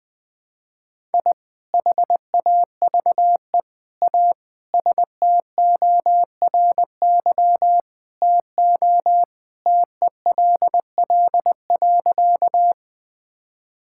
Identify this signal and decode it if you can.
Morse code, heard as a keyed tone, about 20 wpm, 700 Hz. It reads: I HAVE A STORY TO TELL.
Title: Sentences with 2-Word Phrases - 20wpm